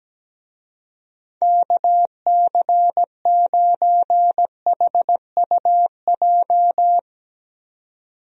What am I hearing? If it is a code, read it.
Morse code, heard as a keyed tone, about 17 wpm, 700 Hz. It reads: KC9HUJ
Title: USA Call Signs - 6 Characters - 17wpm